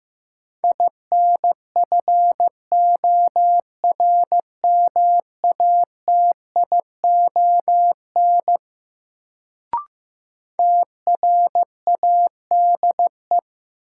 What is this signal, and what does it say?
Morse code, heard as a keyed tone, about 15 wpm, 700 Hz. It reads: INFORMATION TRADE
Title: Sentences from Top 600 Words - New Word Drill - 15wpm